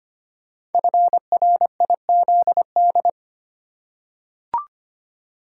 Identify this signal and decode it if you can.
Morse code, heard as a keyed tone, about 25 wpm, 700 Hz. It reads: FRIZD
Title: Code Groups - 5 Letter:Numbers - 25wpm